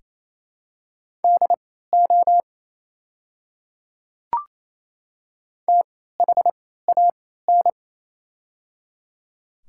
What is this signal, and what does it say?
Morse code, heard as a keyed tone, about 28 wpm, spaced as for 16 wpm, 700 Hz. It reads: DO THAN
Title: 3x Character Spacing - Top 100 Words - 28wpm